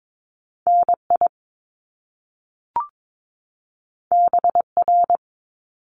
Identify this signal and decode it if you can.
Morse code, heard as a keyed tone, about 22 wpm, 700 Hz. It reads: NI BR
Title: Top Two Letter Combinations in Words 22wpm